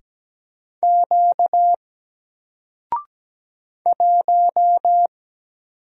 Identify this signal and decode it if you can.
Morse code, heard as a keyed tone, about 17 wpm, 700 Hz. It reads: Q 1
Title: Single Letter-Number Alt 17wpm